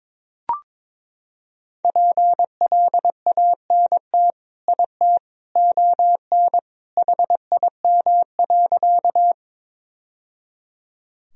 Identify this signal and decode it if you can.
Morse code, heard as a keyed tone, about 21 wpm, 700 Hz. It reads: PLANT IT ON HIM.
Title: Sentences from Top 1000 Words - Review Part 1 - 22wpm